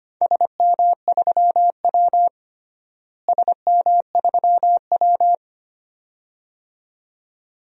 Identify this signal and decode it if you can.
Morse code, heard as a keyed tone, about 25 wpm, 700 Hz. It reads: SM3W SM3W
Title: DXCC-100 Call Signs - 4 Characters - Repeated 2x - 25wpm